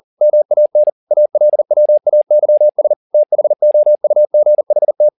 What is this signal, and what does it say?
Morse code, heard as a keyed tone, about 40 wpm, 600 Hz. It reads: MAN ALWAYS THOUGHT